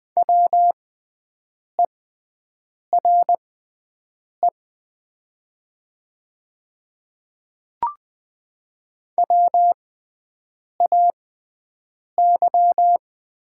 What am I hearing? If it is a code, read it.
Morse code, heard as a keyed tone, about 20 wpm, spaced as for 7 wpm, 690 Hz. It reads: WERE WAY
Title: 6x Character Spacing - Top 100 Words - 20wpm